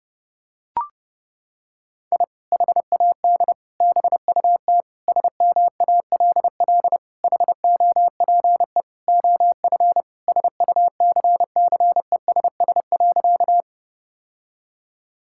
Morse code, heard as a keyed tone, about 30 wpm, 700 Hz. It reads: I HAD BUT SMALL HOPE OF SUCCESS.
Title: Sentences from Top 1000 Words - Review Part 6 - 30wpm